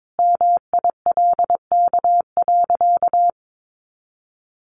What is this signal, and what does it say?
Morse code, heard as a keyed tone, about 22 wpm, 700 Hz. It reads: MILK.
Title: Sentences from Top 1500 Words 22wpm